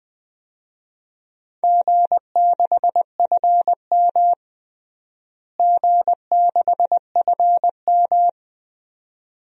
Morse code, heard as a keyed tone, about 20 wpm, 700 Hz. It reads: G6FM G6FM
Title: DXCC-100 Call Signs - 4 Characters - Repeated 2x - 20wpm